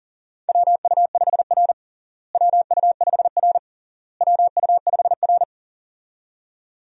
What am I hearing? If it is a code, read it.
Morse code, heard as a keyed tone, about 40 wpm, 700 Hz. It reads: WU5R WU5R WU5R
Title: USA Call Signs - 4 Characters - Repeated 3x - 40wpm